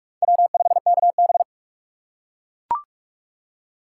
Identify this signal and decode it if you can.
Morse code, heard as a keyed tone, about 44 wpm, 700 Hz. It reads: W5KB
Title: USA Call Signs - 4 Characters - 2x Character Spacing - 45wpm